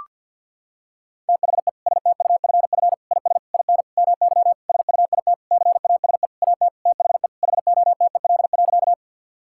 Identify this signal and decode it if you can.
Morse code, heard as a keyed tone, about 50 wpm, 700 Hz. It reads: THE STUFF IS IN MY SUIT CASE AT THE HOTEL.
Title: Sentences from Top 1600 Words 50wpm